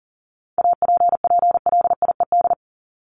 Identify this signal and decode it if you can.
Morse code, heard as a keyed tone, about 40 wpm, 700 Hz. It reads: APPLIED
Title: ED Suffix - Top 100 Words with Base Words 40wpm